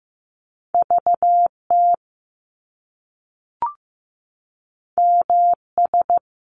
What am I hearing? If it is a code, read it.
Morse code, heard as a keyed tone, about 15 wpm, 700 Hz. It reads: VT MS
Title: US States 15wpm